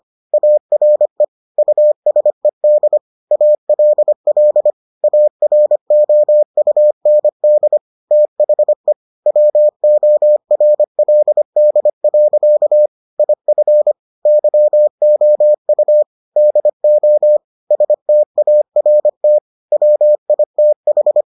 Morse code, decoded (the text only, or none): ARE USED ALL AROUND THE WORLD. IF YOU DO START WITH